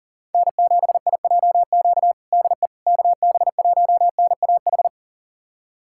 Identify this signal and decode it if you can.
Morse code, heard as a keyed tone, about 40 wpm, 700 Hz. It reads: N7IJQ DE KB1NAH
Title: USA Callsign DE Callsign 40wpm